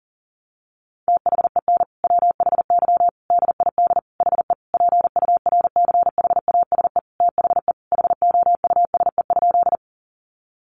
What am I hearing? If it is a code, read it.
Morse code, heard as a keyed tone, about 40 wpm, 700 Hz. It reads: THEN WHY DID HE PURCHASE THE HOUSE?
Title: Sentences from Top 800 Words 40wpm